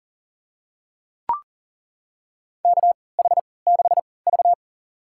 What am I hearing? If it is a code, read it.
Morse code, heard as a keyed tone, about 40 wpm, spaced as for 23 wpm, 700 Hz. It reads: KH6V
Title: USA Call Signs - 3x Character Spacing - 40wpm